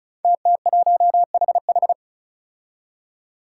Morse code, heard as a keyed tone, about 36 wpm, 700 Hz. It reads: TT1HH
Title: Five Challenging Letter-Number Combinations 35wpm